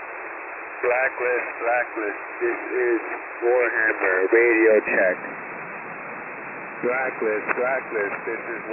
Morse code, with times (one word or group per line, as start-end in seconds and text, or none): none